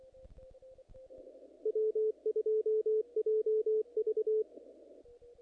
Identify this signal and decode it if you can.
Morse code, heard as a keyed tone, about 23 wpm, 470 Hz, with radio noise. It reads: W2JV
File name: contests / naqp-cw-k5tr-op-k5pi- -1900